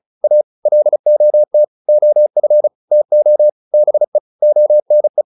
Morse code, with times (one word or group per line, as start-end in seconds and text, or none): 0.2-0.4 A
0.7-1.6 LOT
1.9-2.7 OF
2.9-3.5 TO
3.7-4.2 BE
4.4-5.2 ONE